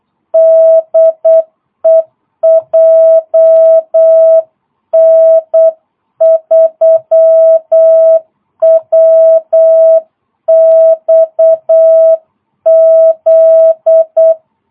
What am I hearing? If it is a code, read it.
Morse code, heard as a keyed tone, about 8 wpm, 650 Hz. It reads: DE JN3WXZ
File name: jh8cbh / 2022cw